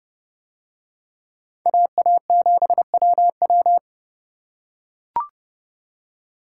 Morse code, heard as a keyed tone, about 28 wpm, 700 Hz. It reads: AA7WW